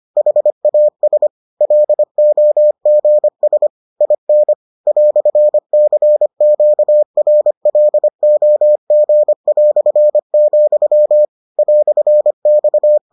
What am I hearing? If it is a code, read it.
Morse code, heard as a keyed tone, about 25 wpm, 600 Hz. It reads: HAS LOGS IN "CQRLOG", "X